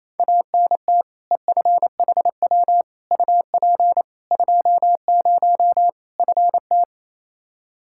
Morse code, heard as a keyed tone, about 28 wpm, 700 Hz. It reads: ANT EFHW UP 20 FT
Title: Practice QSO - Exchange 2 - Info - 28wpm